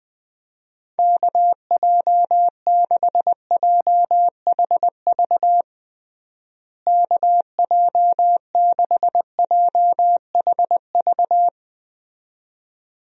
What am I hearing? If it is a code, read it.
Morse code, heard as a keyed tone, about 20 wpm, 700 Hz. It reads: KJ6JHV KJ6JHV
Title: USA Call Signs - 6 Characters - Repeated 2x - 20wpm